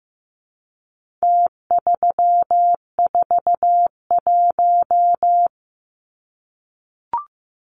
Morse code, heard as a keyed tone, about 15 wpm, 700 Hz. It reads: T341
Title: Numbers - Contest Style - 4-Digit 15wpm